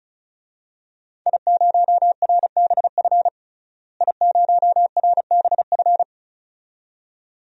Morse code, heard as a keyed tone, about 34 wpm, 700 Hz. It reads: I0RBF I0RBF
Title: DXCC-100 Call Signs - 5 Characters - Repeated 2x - 35wpm